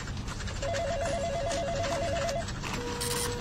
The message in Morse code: none